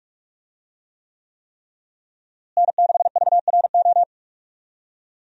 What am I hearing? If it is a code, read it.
Morse code, heard as a keyed tone, about 46 wpm, 700 Hz. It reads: N6VRO
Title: USA Call Signs - 5 Characters - Repeated 3x - 45wpm